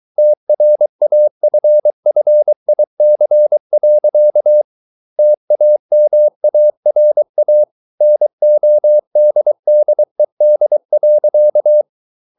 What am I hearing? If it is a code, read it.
Morse code, heard as a keyed tone, about 23 wpm, 600 Hz. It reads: TRAFFIC. TAMARA NODDED.